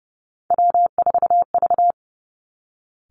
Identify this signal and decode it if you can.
Morse code, heard as a keyed tone, about 31 wpm, 700 Hz. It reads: W4V
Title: Three Challenging Letter-Number Combinations 30wpm